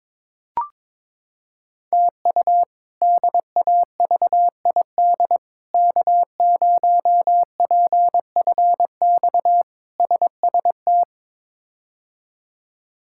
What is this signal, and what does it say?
Morse code, heard as a keyed tone, about 22 wpm, 700 Hz. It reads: TU DAVID K0PFX SST
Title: K1USN SST QSO Practice - 22wpm